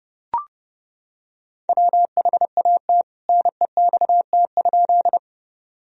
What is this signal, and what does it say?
Morse code, heard as a keyed tone, about 32 wpm, 700 Hz. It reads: WHAT NEXT?
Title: Sentences from Top 1000 Words - Review Part 1 - 30wpm